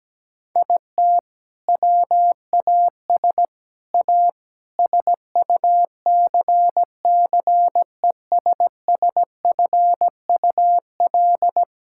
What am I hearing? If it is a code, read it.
Morse code, heard as a keyed tone, about 17 wpm, 700 Hz. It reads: IT WAS A SUCCESSFUL
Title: Sentences from Top 1000 Words - Review Part 6 - 17wpm